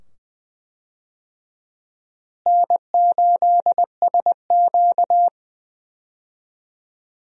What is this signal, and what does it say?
Morse code, heard as a keyed tone, about 20 wpm, 700 Hz. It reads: N8SQ
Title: USA Call Signs - 4 Characters - Repeated 2x - 20wpm